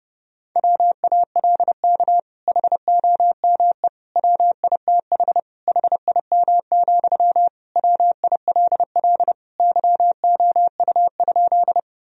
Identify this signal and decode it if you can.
Morse code, heard as a keyed tone, about 30 wpm, 700 Hz. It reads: WALK HOME WITH HIM, WILL YOU?
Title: Sentences from Top 1000 Words - Review Part 1 - 30wpm